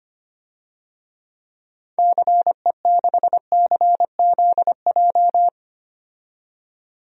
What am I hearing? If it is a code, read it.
Morse code, heard as a keyed tone, about 25 wpm, 700 Hz. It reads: CE6CZJ
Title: DXCC-100 Call Signs - 6 Characters - 25wpm